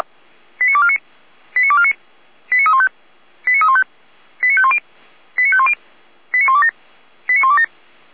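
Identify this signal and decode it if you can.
Five-tone selective call (ZVEI), heard as quick runs of five tones, tones of about 70 ms. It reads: 89249 89249 89315 89315 89510 89510 89127 89127
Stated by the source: ZVEI five-tone selective call received on an SDR, from the feuerwehr-orlamuende fire brigade audio archive.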